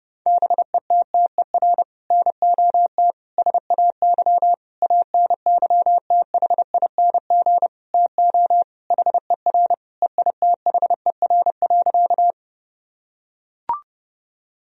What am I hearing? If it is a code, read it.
Morse code, heard as a keyed tone, about 30 wpm, 700 Hz. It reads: BETTER NOT SAY ANYTHING TO HER EITHER.
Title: Sentences from Top 1000 Words - Review Part 4 - 30wpm